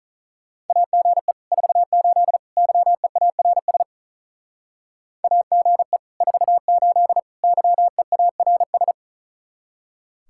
Morse code, heard as a keyed tone, about 43 wpm, 700 Hz. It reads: AGE 48 YEARS AGE 48 YEARS
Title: Speed-Racing Common QSO Phrases Encore - 35wpm